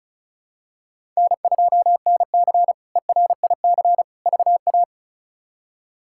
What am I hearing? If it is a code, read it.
Morse code, heard as a keyed tone, about 34 wpm, 700 Hz. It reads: N2NC ERIC VA
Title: K1USN SST QSO Practice - 35wpm